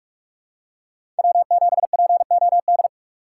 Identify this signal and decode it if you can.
Morse code, heard as a keyed tone, about 41 wpm, 700 Hz. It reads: W7POD
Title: USA Call Signs - 5 Characters - 4x Character Spacing - 45wpm